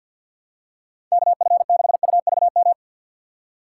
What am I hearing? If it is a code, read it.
Morse code, heard as a keyed tone, about 47 wpm, 700 Hz. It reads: KF6UVM
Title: USA Call Signs - 6x Character Spacing - 50wpm